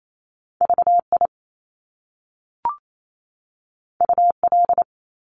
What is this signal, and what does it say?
Morse code, heard as a keyed tone, about 28 wpm, 700 Hz. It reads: VI UL
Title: Top Two Letter Combinations in Words 28wpm